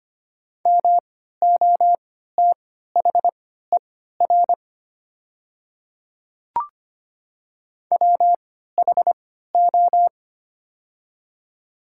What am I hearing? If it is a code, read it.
Morse code, heard as a keyed tone, about 25 wpm, spaced as for 14 wpm, 690 Hz. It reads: MOTHER WHO
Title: 3x Character Spacing - Top 200 Words - 25wpm